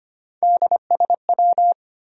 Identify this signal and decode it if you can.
Morse code, heard as a keyed tone, about 34 wpm, spaced as for 30 wpm, 700 Hz. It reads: DSW
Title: Speed-Racing Abbreviations - 25wpm